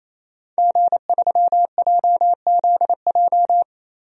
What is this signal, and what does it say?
Morse code, heard as a keyed tone, about 28 wpm, 700 Hz. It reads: G3JZJ
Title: DXCC-100 Call Signs - 5 Characters - 28wpm